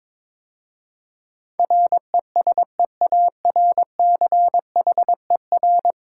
Morse code, heard as a keyed tone, about 22 wpm, 700 Hz. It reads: RESEARCHER
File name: practice / 22wpm / 10 Letter Words - First 3 Letters Given - 22wpm